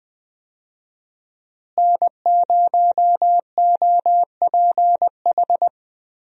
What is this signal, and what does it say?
Morse code, heard as a keyed tone, about 20 wpm, 700 Hz. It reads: N0OPH
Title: USA Call Signs - 5 Characters - Repeated 3x - 20wpm